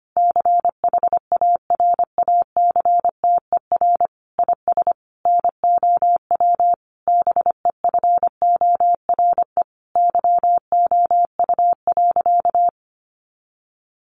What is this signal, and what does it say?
Morse code, heard as a keyed tone, about 25 wpm, 700 Hz. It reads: CHARACTER IS NOW BEFORE YOU.
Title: Sentences from Top 1200 Words 25wpm